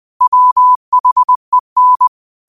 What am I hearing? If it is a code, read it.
Morse code, heard as a keyed tone, about 20 wpm, 1000 Hz. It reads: WHEN